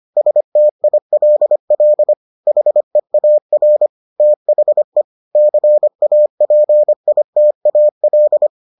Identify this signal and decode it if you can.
Morse code, heard as a keyed tone, about 25 wpm, 600 Hz. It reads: STILL HEAR THE CAPITAL